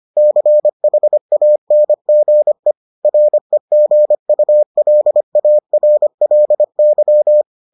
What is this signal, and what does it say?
Morse code, heard as a keyed tone, about 25 wpm, 600 Hz. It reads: CHANGE REGULARLY